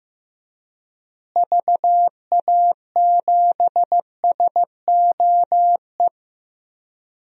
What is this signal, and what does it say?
Morse code, heard as a keyed tone, about 15 wpm, 700 Hz. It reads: VA7SOE
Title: DXCC-100 Call Signs - 6 Characters - Repeated 2x - 15wpm